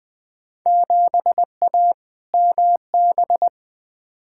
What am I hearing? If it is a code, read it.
Morse code, heard as a keyed tone, about 20 wpm, 700 Hz. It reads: 7A MB
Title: Field Day Exchanges Encore - 20wpm